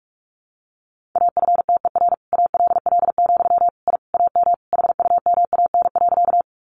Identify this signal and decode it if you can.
Morse code, heard as a keyed tone, about 44 wpm, 700 Hz. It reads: AFTER ALL, I AM HUMAN.
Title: Sentences from Top 600 Words 45wpm